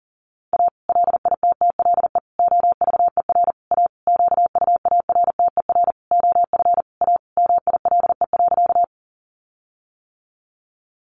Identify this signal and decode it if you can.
Morse code, heard as a keyed tone, about 40 wpm, 700 Hz. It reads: A LITTLE OVER A QUARTER OF A MILE.
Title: Sentences from Top 1300 Words 40wpm